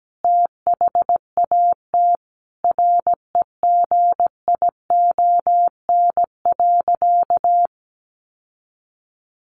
Morse code, heard as a keyed tone, about 17 wpm, 700 Hz. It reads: THAT REGION.
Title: Sentences from Top 1500 Words 17wpm